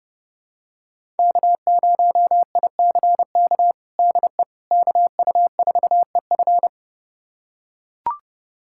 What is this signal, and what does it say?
Morse code, heard as a keyed tone, about 32 wpm, 700 Hz. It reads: K0ICK DE KU4EF
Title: USA Callsign DE Callsign - 5 Characters - 30wpm